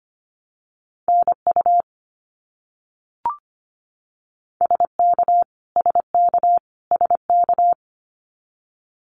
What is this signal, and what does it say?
Morse code, heard as a keyed tone, about 25 wpm, 700 Hz. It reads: NU SK SK SK
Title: Canadian Provinces and Territories - Repeated 3x - 25wpm